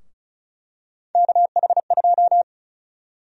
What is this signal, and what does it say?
Morse code, heard as a keyed tone, about 36 wpm, 700 Hz. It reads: KH2
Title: USA Call Sign Prefixes - 35wpm